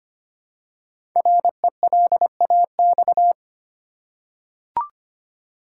Morse code, heard as a keyed tone, about 25 wpm, 700 Hz. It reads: RELAX
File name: 5 Letter Words - First 3 Letters Given - 25wpm